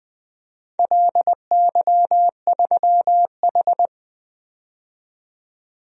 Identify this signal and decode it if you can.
Morse code, heard as a keyed tone, about 20 wpm, 700 Hz. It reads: LY3H
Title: DXCC-100 Call Signs - 4 Characters - 20wpm